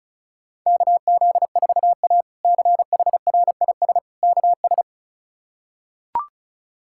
Morse code, heard as a keyed tone, about 34 wpm, 700 Hz. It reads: KZ4A CHRIS KS
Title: K1USN SST QSO Practice - 35wpm